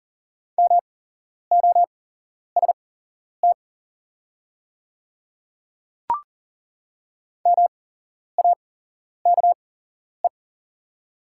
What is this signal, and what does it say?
Morse code, heard as a keyed tone, about 40 wpm, spaced as for 11 wpm, 690 Hz. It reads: MOST MAKE
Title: 8x Character Spacing - Top 100 Words - 40wpm